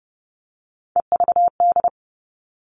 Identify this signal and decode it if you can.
Morse code, heard as a keyed tone, about 30 wpm, 700 Hz. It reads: EVD
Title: All Cut Numbers - 3-Digit 30wpm